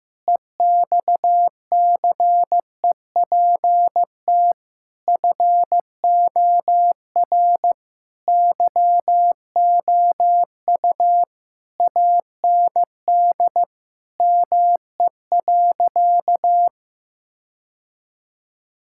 Morse code, heard as a keyed tone, about 15 wpm, 700 Hz. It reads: EXCEPT FOR YOU AND ME.
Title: Sentences from Top 1600 Words 15wpm